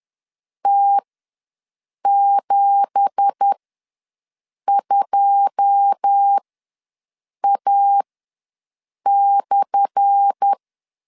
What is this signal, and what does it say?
Morse code, heard as a keyed tone, about 10 wpm, spaced as for 6 wpm, 800 Hz. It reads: T72A/
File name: LEZIONI / 10 SIGNS / 10A SLASH ws 10wpm 4121